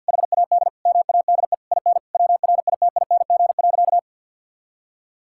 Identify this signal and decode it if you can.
Morse code, heard as a keyed tone, about 50 wpm, 700 Hz. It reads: HAD MADE IN WRITING.
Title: Sentences from Top 1600 Words 50wpm